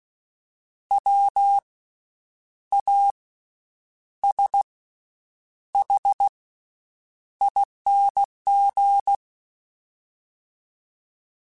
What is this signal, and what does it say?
Morse code, heard as a keyed tone, about 16 wpm, spaced as for 8 wpm, 800 Hz. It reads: WASHING
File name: Special INGs-mp3s / Special INGs-mp3s / Slow ing @15